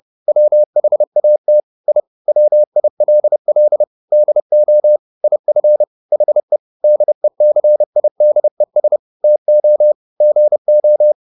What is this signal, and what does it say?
Morse code, heard as a keyed tone, about 30 wpm, 600 Hz. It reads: WHAT I WILL DO IF HE DECIDES TO GO
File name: the old man and the sea / the old man and the sea 1 30wpm